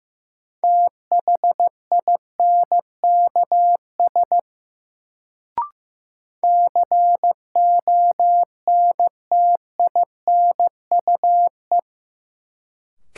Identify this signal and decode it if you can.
Morse code, heard as a keyed tone, about 15 wpm, 700 Hz. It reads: THINKS CONTINUE
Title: Sentences from Top 1000 Words - Review Part 6 - New Word Drill - 15wpm